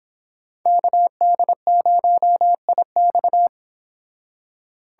Morse code, heard as a keyed tone, about 28 wpm, 700 Hz. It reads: KD0IX